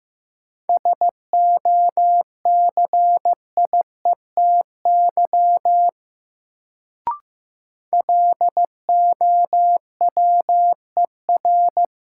Morse code, heard as a keyed tone, about 15 wpm, 700 Hz. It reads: SOCIETY LOWER
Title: Sentences from Top 700 Words - New Word Drill - 15wpm